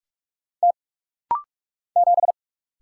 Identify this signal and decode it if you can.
Morse code, heard as a keyed tone, about 44 wpm, 700 Hz. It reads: T 7
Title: Single Letter-Number Alt - Rapid-Fire 45wpm